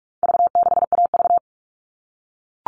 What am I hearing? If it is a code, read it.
Morse code, heard as a keyed tone, about 44 wpm, 700 Hz. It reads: V6AV
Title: All Cut Numbers - 4-Digit 45wpm